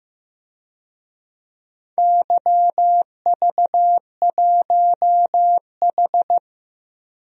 Morse code, heard as a keyed tone, about 15 wpm, 700 Hz. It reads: YV1H
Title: DXCC-100 Call Signs - 4 Characters - 15wpm